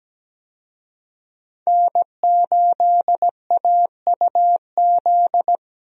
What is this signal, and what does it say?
Morse code, heard as a keyed tone, about 17 wpm, 700 Hz. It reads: N8AUZ
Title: USA Call Signs - 5 Characters - 17wpm